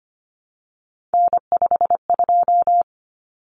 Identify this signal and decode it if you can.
Morse code, heard as a keyed tone, about 25 wpm, 700 Hz. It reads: N52
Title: Common Cut Numbers - 3-Digit 25wpm